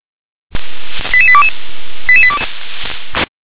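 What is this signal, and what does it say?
Five-tone selective call (ZVEI), heard as quick runs of five tones, tones of about 70 ms. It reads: 88822
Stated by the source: ZVEI five-tone selective call received by SDR.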